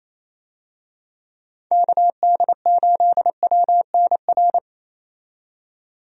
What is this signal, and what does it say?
Morse code, heard as a keyed tone, about 28 wpm, 700 Hz. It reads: KD8WNR
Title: USA Call Signs - 6 Characters - 28wpm